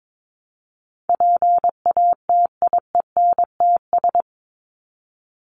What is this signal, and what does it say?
Morse code, heard as a keyed tone, about 22 wpm, 700 Hz. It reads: PATIENTS
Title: ES:S Suffix - Top 100 Words with Base Words 22wpm